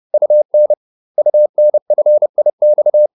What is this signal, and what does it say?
Morse code, heard as a keyed tone, about 30 wpm, 600 Hz. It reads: UN UNFIX